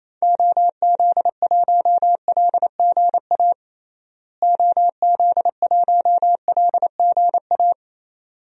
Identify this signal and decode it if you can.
Morse code, heard as a keyed tone, about 28 wpm, 700 Hz. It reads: OZ1LGA OZ1LGA
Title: DXCC-100 Call Signs - 6 Characters - Repeated 2x - 28wpm